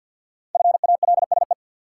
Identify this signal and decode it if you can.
Morse code, heard as a keyed tone, about 51 wpm, 700 Hz. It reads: FALSE
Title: Sentences from Top 1600 Words - New Word Drill Alt - 50wpm